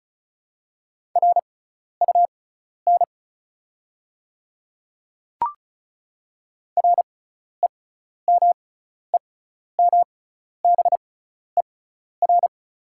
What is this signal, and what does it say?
Morse code, heard as a keyed tone, about 36 wpm, spaced as for 12 wpm, 690 Hz. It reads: RUN REMEMBER
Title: 6x Character Spacing - Top 500 Words - 35wpm